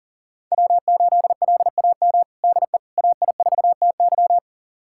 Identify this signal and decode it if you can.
Morse code, heard as a keyed tone, about 40 wpm, 700 Hz. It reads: W8LAM DE AI4TY
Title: USA Callsign DE Callsign - 5 Characters - 40wpm